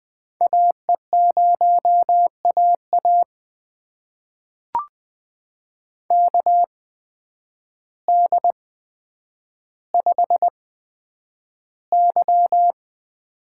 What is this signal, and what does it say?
Morse code, heard as a keyed tone, about 20 wpm, spaced as for 9 wpm, 700 Hz. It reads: AE0AA KD5Y
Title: USA Call Signs - 8x Character Spacing - 20wpm